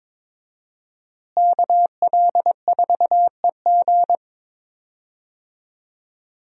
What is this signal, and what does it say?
Morse code, heard as a keyed tone, about 22 wpm, 700 Hz. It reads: KL4EG
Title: USA Call Signs - 5 Characters - Repeated 2x - 22wpm